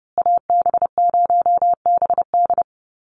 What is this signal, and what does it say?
Morse code, heard as a keyed tone, about 30 wpm, 700 Hz. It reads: AB0BD
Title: USA Call Signs Repeated 2x 30wpm